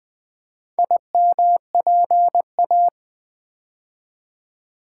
Morse code, heard as a keyed tone, about 20 wpm, 700 Hz. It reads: IMPA
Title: Build A Word - 2 Letters At A Time - 6-Letter Words - 20wpm